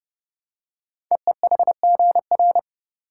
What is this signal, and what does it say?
Morse code, heard as a keyed tone, about 30 wpm, 690 Hz. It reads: EEHGR
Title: Code Groups - 5 Letters - 30wpm